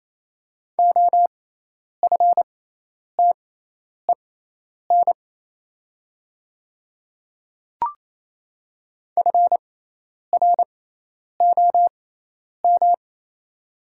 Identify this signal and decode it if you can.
Morse code, heard as a keyed tone, about 27 wpm, spaced as for 10 wpm, 700 Hz. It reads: OFTEN FROM
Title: 6x Character Spacing - Top 500 Words - 28wpm